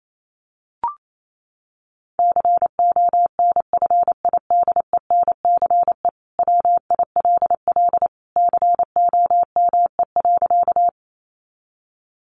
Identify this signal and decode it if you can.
Morse code, heard as a keyed tone, about 28 wpm, 700 Hz. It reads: CONFIDENCE WILL COME.
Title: Sentences from Top 1500 Words 28wpm